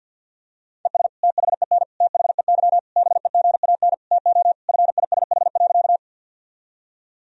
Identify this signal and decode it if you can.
Morse code, heard as a keyed tone, about 47 wpm, 700 Hz. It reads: ES THEN THEY BEGAN TO FISH.
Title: Sentences from Top 300 Words and Abbreviations 50wpm